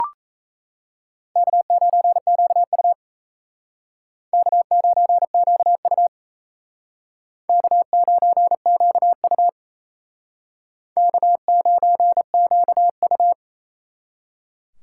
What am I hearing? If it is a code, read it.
Morse code, heard as a keyed tone, about 35 wpm, 700 Hz. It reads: K9QU K9QU K9QU K9QU
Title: Speed Racing - USA 1x2 Call Signs 28wpm